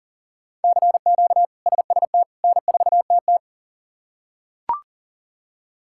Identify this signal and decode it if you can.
Morse code, heard as a keyed tone, about 40 wpm, 700 Hz. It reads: CQ SST N4TT
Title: K1USN SST QSO Practice - 40wpm